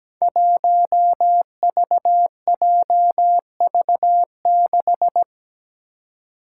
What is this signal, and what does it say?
Morse code, heard as a keyed tone, about 17 wpm, 700 Hz. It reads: 1VJV6
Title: Five Challenging Letter-Number Combinations 17wpm